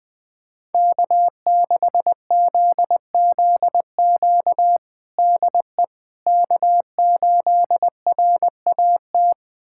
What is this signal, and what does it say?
Morse code, heard as a keyed tone, about 21 wpm, 700 Hz. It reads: K6ZZQ DE K8RAT
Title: USA Callsign DE Callsign 20wpm